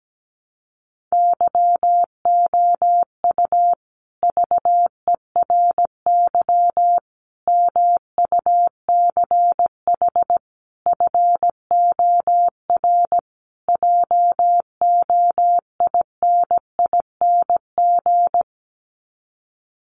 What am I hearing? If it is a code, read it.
Morse code, heard as a keyed tone, about 17 wpm, 700 Hz. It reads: YOU VERY MUCH FOR JOINING
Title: Sets of 5 Words Encore 17wpm